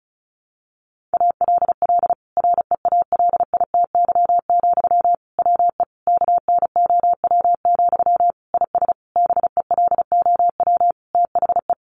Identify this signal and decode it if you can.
Morse code, heard as a keyed tone, about 34 wpm, 700 Hz. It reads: ALL REALITY, WE KNOW, IS BELOW THE
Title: Sentences from Top 1400 Words 35wpm